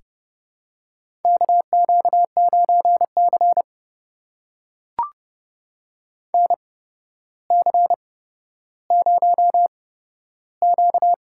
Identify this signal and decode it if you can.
Morse code, heard as a keyed tone, about 32 wpm, spaced as for 13 wpm, 700 Hz. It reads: KQ9C NC0Q
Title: USA Call Signs - 4 Characters - 8x Character Spacing - 30wpm